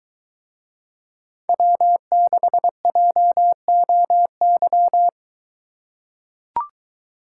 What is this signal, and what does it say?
Morse code, heard as a keyed tone, about 22 wpm, 700 Hz. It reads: W6JOY